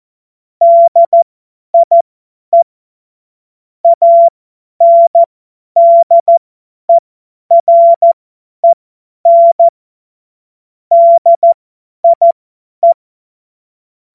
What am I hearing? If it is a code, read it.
Morse code, heard as a keyed tone, about 14 wpm, spaced as for 10 wpm, 680 Hz. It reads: DIE ANDEREN DIE
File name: Morsekurs / Witze / Witz 07. September 2021, 50 F70